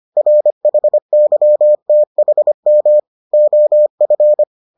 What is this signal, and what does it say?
Morse code, heard as a keyed tone, about 25 wpm, 600 Hz. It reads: RHYTHM OF